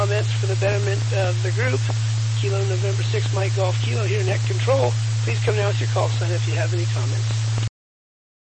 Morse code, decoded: none

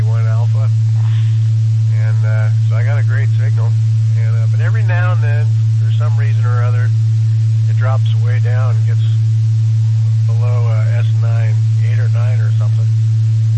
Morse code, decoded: none

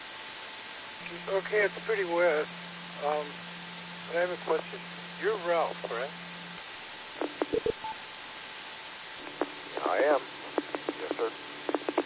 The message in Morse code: none

